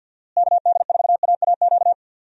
Morse code, heard as a keyed tone, about 51 wpm, 700 Hz. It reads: KD4AAQ
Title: USA Call Signs - 6 Characters - Repeated 3x - 50wpm